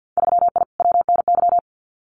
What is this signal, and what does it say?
Morse code, heard as a keyed tone, about 50 wpm, 700 Hz. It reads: 3I WNY
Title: Winter Field Day Exchanges Encore 50wpm